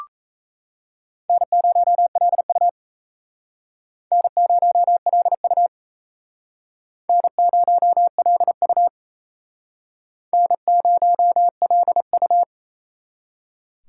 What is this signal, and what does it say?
Morse code, heard as a keyed tone, about 35 wpm, 700 Hz. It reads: N0LU N0LU N0LU N0LU